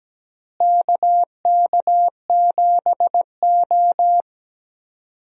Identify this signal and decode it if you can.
Morse code, heard as a keyed tone, about 17 wpm, 700 Hz. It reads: KK7O
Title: USA Call Signs - 4 Characters - Repeated 3x - 17wpm